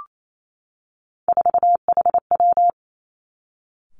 Morse code, heard as a keyed tone, about 28 wpm, 700 Hz. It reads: 4HW
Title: Three Challenging Letter-Number Combinations 28wpm